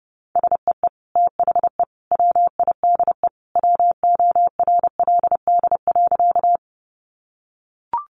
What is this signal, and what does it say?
Morse code, heard as a keyed tone, about 30 wpm, 700 Hz. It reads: SEE THE WIDE WORLD.
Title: Sentences from Top 900 Words 30wpm